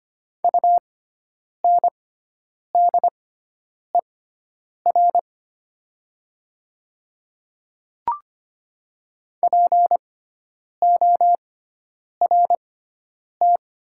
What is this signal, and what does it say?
Morse code, heard as a keyed tone, about 25 wpm, spaced as for 9 wpm, 700 Hz. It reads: UNDER PORT
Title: 6x Character Spacing - Top 200 Words - 25wpm